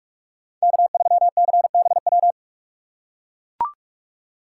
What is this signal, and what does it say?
Morse code, heard as a keyed tone, about 44 wpm, 700 Hz. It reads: K3CBW